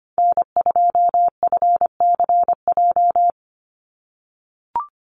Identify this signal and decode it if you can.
Morse code, heard as a keyed tone, about 25 wpm, 700 Hz. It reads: N2FCJ
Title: USA Call Signs 25wpm